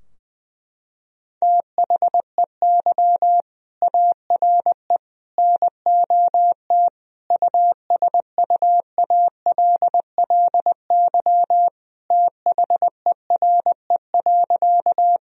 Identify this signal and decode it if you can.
Morse code, heard as a keyed tone, about 20 wpm, 700 Hz. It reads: THEY ARE NOT USUALLY THERE.
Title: Sentences from Top 1000 Words - Review Part 3 - 20wpm